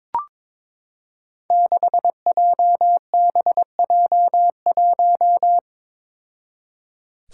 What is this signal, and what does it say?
Morse code, heard as a keyed tone, about 22 wpm, 700 Hz. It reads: 6JBJ1
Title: Five Challenging Letter-Number Combinations 22wpm